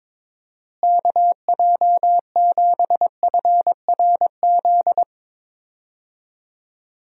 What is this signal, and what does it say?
Morse code, heard as a keyed tone, about 22 wpm, 700 Hz. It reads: KJ7FRZ
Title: USA Call Signs - 6 Characters - Repeated 3x - 22wpm